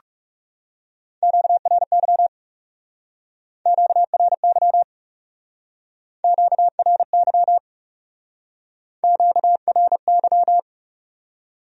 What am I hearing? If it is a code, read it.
Morse code, heard as a keyed tone, about 36 wpm, 700 Hz. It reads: QRY QRY QRY QRY
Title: Speed-Racing Q-Codes - 30wpm